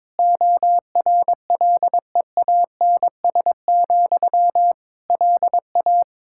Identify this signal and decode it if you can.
Morse code, heard as a keyed tone, about 22 wpm, 700 Hz. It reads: ORLEANS, LA
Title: K1USN SST QSO Practice 22wpm